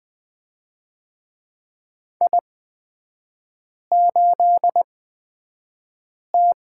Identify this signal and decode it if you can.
Morse code, heard as a keyed tone, about 20 wpm, spaced as for 5 wpm, 700 Hz. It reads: I8T
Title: 243 Lesson 39e - Letter z - Three Characters 20wpm